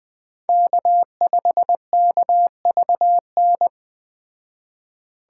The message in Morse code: K5KVN

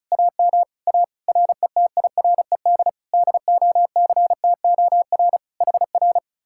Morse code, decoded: AM A RETIRED DOCTOR HR

